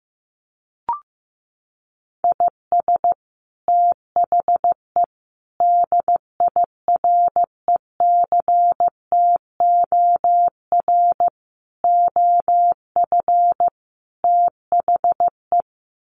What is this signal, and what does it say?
Morse code, heard as a keyed tone, about 15 wpm, 700 Hz. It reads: IS THE DIRECTOR OF THE
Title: Sets of 5 Words Encore2 15wpm